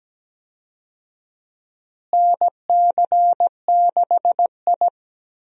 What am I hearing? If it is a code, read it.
Morse code, heard as a keyed tone, about 17 wpm, 700 Hz. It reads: NC6I
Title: USA Call Signs - 4 Characters - Repeated 2x - 17wpm